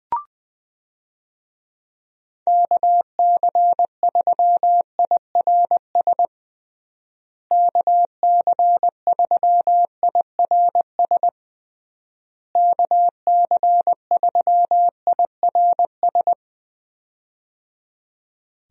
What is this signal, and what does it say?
Morse code, heard as a keyed tone, about 20 wpm, 700 Hz. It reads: KC3IRS KC3IRS KC3IRS
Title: USA Call Signs - 6 Characters - Repeated 3x - 20wpm